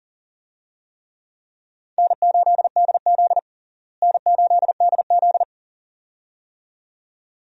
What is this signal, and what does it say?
Morse code, heard as a keyed tone, about 37 wpm, 700 Hz. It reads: N8DZ N8DZ
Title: USA Call Signs - 4 Characters - Repeated 2x - 40wpm